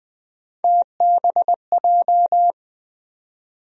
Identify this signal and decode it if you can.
Morse code, heard as a keyed tone, about 21 wpm, 700 Hz. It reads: TBJ
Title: USA Call Sign Postfixes - 3 Letters 20wpm